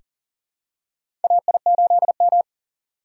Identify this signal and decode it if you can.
Morse code, heard as a keyed tone, about 40 wpm, 700 Hz. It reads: AI8M
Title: USA Call Signs - 4 Characters - 4x Character Spacing - 40wpm